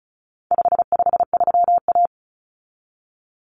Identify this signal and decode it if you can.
Morse code, heard as a keyed tone, about 34 wpm, 700 Hz. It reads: 553A
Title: Common Cut Numbers - 4-Digit 35wpm